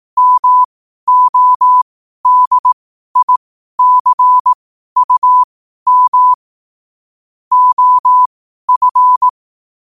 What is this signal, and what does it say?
Morse code, heard as a keyed tone, about 18 wpm, spaced as for 12 wpm, 1000 Hz. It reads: MODICUM OF
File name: qotdfiles / 12wpm